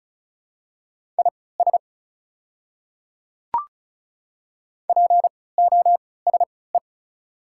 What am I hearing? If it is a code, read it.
Morse code, heard as a keyed tone, about 36 wpm, spaced as for 20 wpm, 690 Hz. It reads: IS POSE
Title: 3x Character Spacing - Top 500 Words - 35wpm